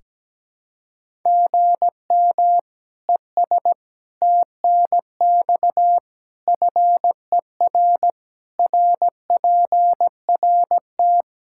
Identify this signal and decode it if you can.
Morse code, heard as a keyed tone, about 17 wpm, 700 Hz. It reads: GM ES TNX FER RPRT
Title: K1USN SST QSO Practice 17wpm